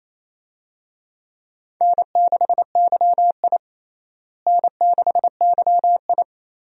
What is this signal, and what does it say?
Morse code, heard as a keyed tone, about 27 wpm, 700 Hz. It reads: N6YI N6YI
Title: USA Call Signs - 4 Characters - Repeated 2x - 28wpm